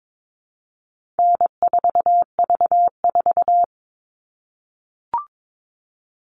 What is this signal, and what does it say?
Morse code, heard as a keyed tone, about 22 wpm, 700 Hz. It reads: N4V4
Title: All Cut Numbers - 4-Digit 22wpm